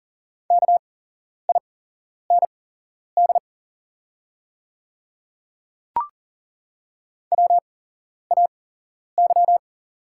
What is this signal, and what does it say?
Morse code, heard as a keyed tone, about 40 wpm, spaced as for 11 wpm, 700 Hz. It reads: KIND WAY
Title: 8x Character Spacing - Top 200 Words - 40wpm